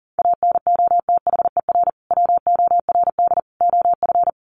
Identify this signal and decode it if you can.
Morse code, heard as a keyed tone, about 40 wpm, 700 Hz. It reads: ANOTHER WORD OF
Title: Sentences from Top 1100 Words 40wpm